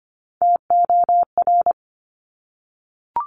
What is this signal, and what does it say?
Morse code, heard as a keyed tone, about 25 wpm, 700 Hz. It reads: TOR
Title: Top Three Letter Combinations in Words 25wpm